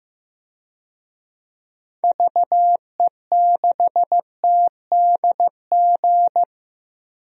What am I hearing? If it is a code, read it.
Morse code, heard as a keyed tone, about 15 wpm, 700 Hz. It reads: VE6TDG
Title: DXCC-100 Call Signs - 6 Characters - 15wpm